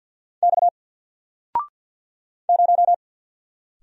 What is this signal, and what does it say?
Morse code, heard as a keyed tone, about 50 wpm, 700 Hz. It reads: X 0
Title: Single Letter-Number 50wpm